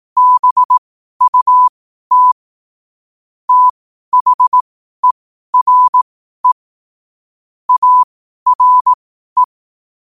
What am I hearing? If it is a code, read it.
Morse code, heard as a keyed tone, about 18 wpm, spaced as for 12 wpm, 1000 Hz. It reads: BUT THERE ARE
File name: qotdfiles / 12wpm